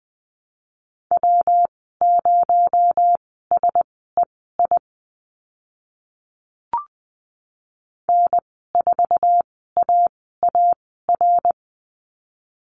Morse code, heard as a keyed tone, about 20 wpm, spaced as for 14 wpm, 700 Hz. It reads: W0SEI N4AAR
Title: 219 Lesson 35g - Number 0 - Call Signs 20wpm